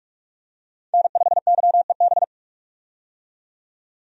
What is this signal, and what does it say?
Morse code, heard as a keyed tone, about 46 wpm, 700 Hz. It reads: N5YEB
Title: USA Call Signs - 5 Characters - 45wpm